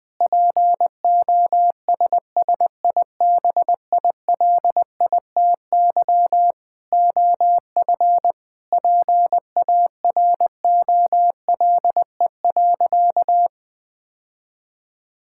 Morse code, heard as a keyed tone, about 20 wpm, 700 Hz. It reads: POSSIBILITY OF PAROLE.
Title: Sentences with 8-Word Phrases - Spoken Beginning - 20wpm